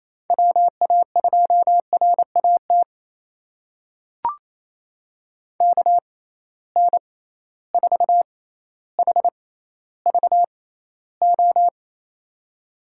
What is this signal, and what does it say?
Morse code, heard as a keyed tone, about 28 wpm, spaced as for 14 wpm, 700 Hz. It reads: WA2RAT KN4HVO
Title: USA Call Signs - 6 Characters - 6x Character Spacing - 28wpm